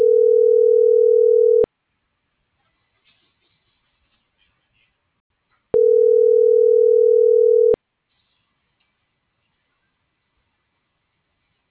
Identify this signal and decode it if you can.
Ringback tone (North America), 440 + 480 Hz.